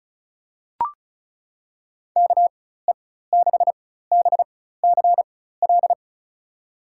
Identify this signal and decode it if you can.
Morse code, heard as a keyed tone, about 36 wpm, spaced as for 16 wpm, 700 Hz. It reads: KE6BCL
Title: USA Call Signs - 6 Characters - 4x Character Spacing - 35wpm